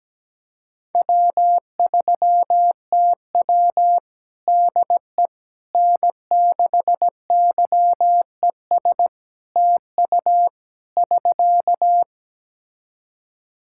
Morse code, heard as a keyed tone, about 17 wpm, 700 Hz. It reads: W3TW DE N6YES TU <SK>
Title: Practice QSO - Exchange 4 - Ending - 17wpm